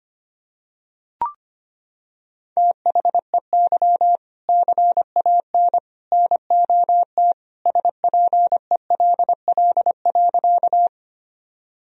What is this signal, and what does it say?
Morse code, heard as a keyed tone, about 25 wpm, 700 Hz. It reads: THEY CAN NOT SPELL.
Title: Sentences from Top 1000 Words - Review Part 1 - 25wpm